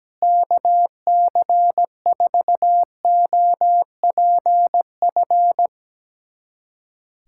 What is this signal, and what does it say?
Morse code, heard as a keyed tone, about 17 wpm, 700 Hz. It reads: KC4OPF
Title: USA Callsign DE Callsign 17wpm